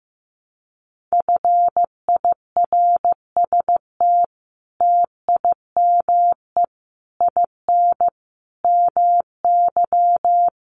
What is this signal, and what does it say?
Morse code, heard as a keyed tone, about 15 wpm, 700 Hz. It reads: FIRST TIME IN MY
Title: Sets of 5 Words Encore2 15wpm